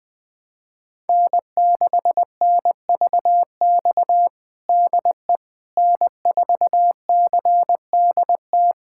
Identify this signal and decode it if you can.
Morse code, heard as a keyed tone, about 20 wpm, 700 Hz. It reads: N6NVX DE N4CDT
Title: USA Callsign DE Callsign - 5 Characters - 20wpm